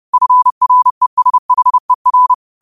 Morse code, heard as a keyed tone, about 30 wpm, 1000 Hz. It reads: FRESHER
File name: qotdfiles / 30wpm